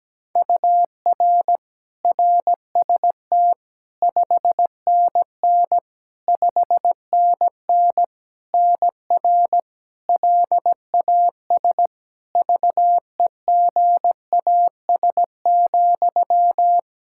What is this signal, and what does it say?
Morse code, heard as a keyed tone, about 17 wpm, 700 Hz. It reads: UR RST 5NN 5NN NR LAS VEGAS,
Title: K1USN SST QSO Practice 17wpm